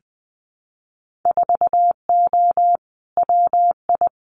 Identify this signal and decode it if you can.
Morse code, heard as a keyed tone, about 20 wpm, 700 Hz. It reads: 4O WI
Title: Winter Field Day Exchanges 20wpm